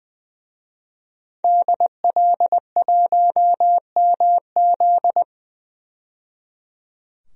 Morse code, heard as a keyed tone, about 20 wpm, 700 Hz. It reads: DL1MZ
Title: DXCC-100 Call Signs - 5 Characters - 20wpm